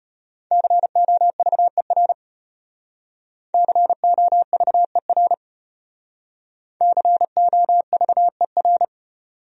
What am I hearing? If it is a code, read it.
Morse code, heard as a keyed tone, about 33 wpm, 700 Hz. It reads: COVER COVER COVER